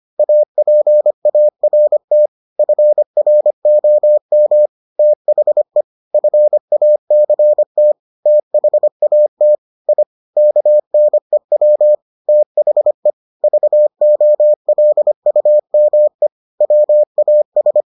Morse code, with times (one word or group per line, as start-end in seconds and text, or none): 0.2-2.3 APART
2.6-4.7 FROM
5.0-5.8 THE
6.1-7.9 FACT
8.3-9.6 THAT
9.9-10.0 I
10.4-11.9 KNEW
12.3-13.1 THE
13.4-16.3 VOLUME
16.6-17.8 WAS